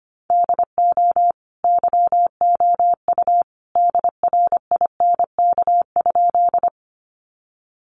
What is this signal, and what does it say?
Morse code, heard as a keyed tone, about 25 wpm, 700 Hz. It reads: DO YOU DRINK?
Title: Sentences from Top 1500 Words 25wpm